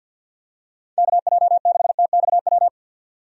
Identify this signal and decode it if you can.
Morse code, heard as a keyed tone, about 50 wpm, 700 Hz. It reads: KJ6TXW
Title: USA Call Signs - 6 Characters - 50wpm